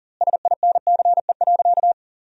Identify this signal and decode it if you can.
Morse code, heard as a keyed tone, about 40 wpm, 700 Hz. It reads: SINCE.
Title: 2x Word-Spacing - Sentences from Top 400 Words - 40wpm